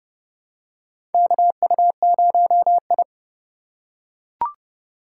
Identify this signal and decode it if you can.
Morse code, heard as a keyed tone, about 30 wpm, 700 Hz. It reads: KU0I
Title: USA Call Signs - 4 Characters - 2x Character Spacing - 30wpm